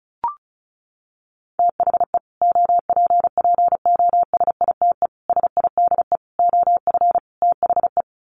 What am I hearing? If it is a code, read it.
Morse code, heard as a keyed tone, about 34 wpm, 700 Hz. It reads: THE OPPOSITE SIDE OF THE
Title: Sets of 5 Words Encore 35wpm